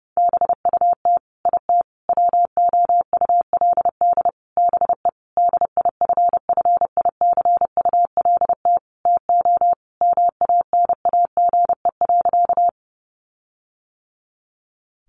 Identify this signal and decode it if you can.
Morse code, heard as a keyed tone, about 30 wpm, 700 Hz. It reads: BUT IT WOULD BE DIFFICULT TO MANAGE.
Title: Sentences from Top 1400 Words 30wpm